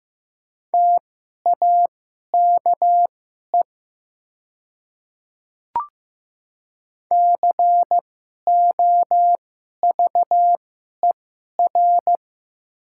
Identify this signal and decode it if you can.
Morse code, heard as a keyed tone, about 15 wpm, spaced as for 11 wpm, 700 Hz. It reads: TAKE COVER
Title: 2x Character Spacing - Top 300 Words - 15wpm